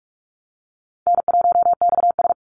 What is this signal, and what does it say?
Morse code, heard as a keyed tone, about 44 wpm, 700 Hz. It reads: N1XS
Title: Easy Call Signs Repeated 3x - 45wpm